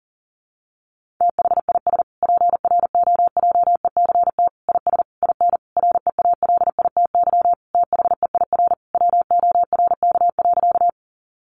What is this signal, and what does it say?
Morse code, heard as a keyed tone, about 40 wpm, 700 Hz. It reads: THIS PROJECT IS IN REALITY THEIR WORK.
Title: Sentences from Top 1400 Words 40wpm